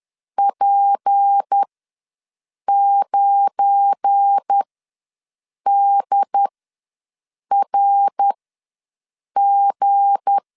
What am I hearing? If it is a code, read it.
Morse code, heard as a keyed tone, about 10 wpm, spaced as for 6 wpm, 800 Hz. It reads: P9DRG
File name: LEZIONI / 7 GF9Z / 7 GF9 ws 10wpm 4034